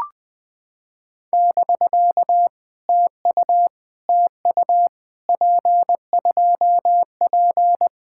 Morse code, heard as a keyed tone, about 20 wpm, 700 Hz. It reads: <BK> TU TU P2P
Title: POTA QSO Practice - 20wpm